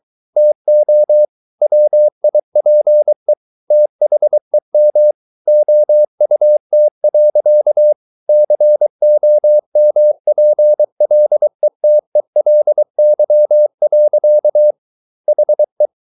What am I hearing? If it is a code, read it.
Morse code, heard as a keyed tone, about 23 wpm, 600 Hz. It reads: TO WIPE THEM OUT. COMPLETELY. HE